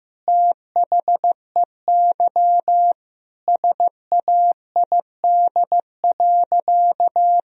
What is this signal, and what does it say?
Morse code, heard as a keyed tone, about 15 wpm, 700 Hz. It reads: THEY SAID.
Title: Sentences from Top 1000 Words - Review Part 3 - 15wpm